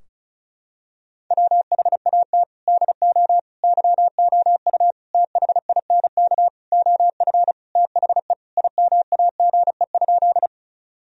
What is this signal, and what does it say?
Morse code, heard as a keyed tone, about 34 wpm, 700 Hz. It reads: WHAT DO YOU THINK OF THE IMAGE?
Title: Sentences from Top 1000 Words - Review Part 4 - 35wpm